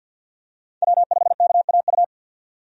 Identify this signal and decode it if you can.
Morse code, heard as a keyed tone, about 48 wpm, 700 Hz. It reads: W5KAU
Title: USA Call Signs - 5 Characters - 50wpm